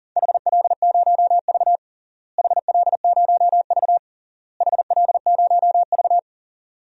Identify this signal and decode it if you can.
Morse code, heard as a keyed tone, about 40 wpm, 700 Hz. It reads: HL0V HL0V HL0V
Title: DXCC-100 Call Signs - 4 Characters - Repeated 3x - 40wpm